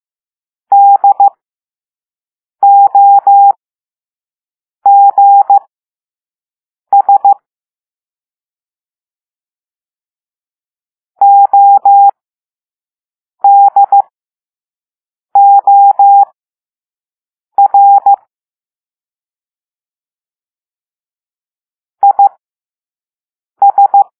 Morse code, heard as a keyed tone, about 15 wpm, spaced as for 5 wpm, 800 Hz. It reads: DOGS ODOR IS
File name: Morse / Level5-All-B